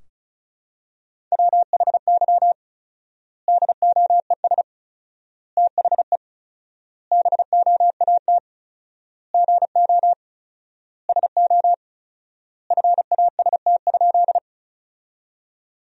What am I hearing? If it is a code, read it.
Morse code, heard as a keyed tone, about 36 wpm, 700 Hz. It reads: WHY DOES THE BOAT GO SO FAST?
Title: 4x Word-Spacing - Sentences from Top 500 Words - 35wpm